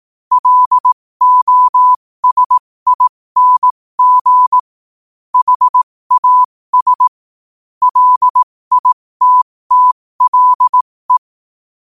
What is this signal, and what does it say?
Morse code, heard as a keyed tone, about 18 wpm, spaced as for 15 wpm, 1000 Hz. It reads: LOSING HAS LITTLE
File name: 15wpm